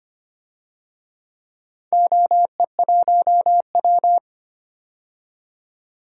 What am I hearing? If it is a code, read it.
Morse code, heard as a keyed tone, about 25 wpm, 700 Hz. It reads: OE1W
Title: DXCC-100 Call Signs - 4 Characters - 25wpm